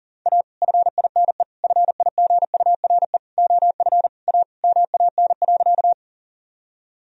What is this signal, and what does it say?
Morse code, heard as a keyed tone, about 40 wpm, 700 Hz. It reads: A FINE FIGURE OF A MAN.
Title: Sentences from Top 1000 Words - Review Part 2 - 40wpm